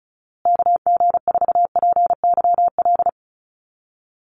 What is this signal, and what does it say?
Morse code, heard as a keyed tone, about 34 wpm, 700 Hz. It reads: KG4PYL
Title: USA Call Signs Repeated 2x 35wpm